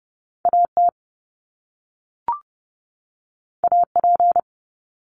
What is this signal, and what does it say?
Morse code, heard as a keyed tone, about 30 wpm, 700 Hz. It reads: A T A P
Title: Top Two Letter Combinations in Words 30wpm